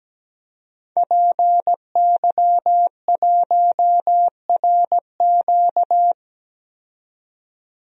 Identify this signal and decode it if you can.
Morse code, heard as a keyed tone, about 17 wpm, 700 Hz. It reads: PY1RQ
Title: DXCC-100 Call Signs - 5 Characters - 17wpm